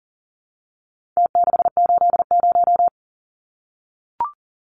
Morse code, heard as a keyed tone, about 40 wpm, 700 Hz. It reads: T680
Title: Numbers - Contest Style - 4-Digit 40wpm